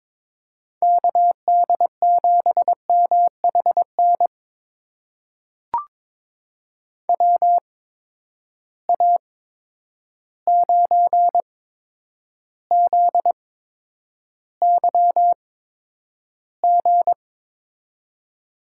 Morse code, heard as a keyed tone, about 22 wpm, spaced as for 9 wpm, 700 Hz. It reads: KD7MHN WA9ZYG
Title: USA Call Signs - 6 Characters - 8x Character Spacing - 22wpm